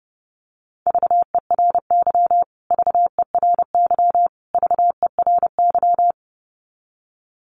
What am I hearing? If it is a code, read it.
Morse code, heard as a keyed tone, about 30 wpm, 700 Hz. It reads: VERY VERY VERY
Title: Top 300 Words Repeated 30wpm